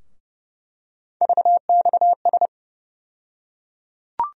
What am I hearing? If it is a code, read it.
Morse code, heard as a keyed tone, about 30 wpm, 700 Hz. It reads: VXS
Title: USA Call Sign Postfixes - 3 Letters 30wpm